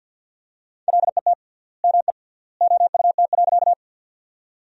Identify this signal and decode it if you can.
Morse code, heard as a keyed tone, about 51 wpm, 700 Hz. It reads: LET ME OUT.
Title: 3x Word-Spacing - Sentences from Top 300 Words - 50wpm